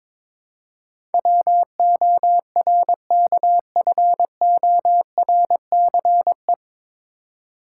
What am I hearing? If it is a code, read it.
Morse code, heard as a keyed tone, about 22 wpm, 700 Hz. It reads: WORKFORCE
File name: 9 Letter Words - First 3 Letters Given - 22wpm